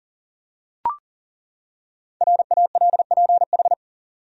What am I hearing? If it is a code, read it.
Morse code, heard as a keyed tone, about 40 wpm, 700 Hz. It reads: RALPH
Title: USA Common Names - 40wpm